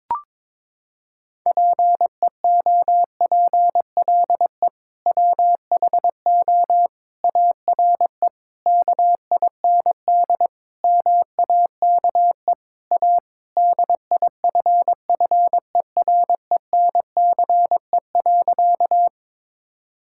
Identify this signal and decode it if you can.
Morse code, heard as a keyed tone, about 22 wpm, 700 Hz. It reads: PEOPLE WHO ARE KIND MAKE A DIFFERENCE.
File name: practice / 22wpm / Sentences with 3-Word Phrases - 22wpm